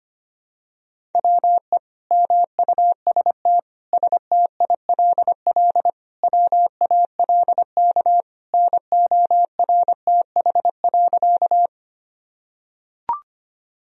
Morse code, heard as a keyed tone, about 25 wpm, 700 Hz. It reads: WE MUST STILL WALK NORTH.
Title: Sentences from Top 1000 Words - Review Part 1 - 25wpm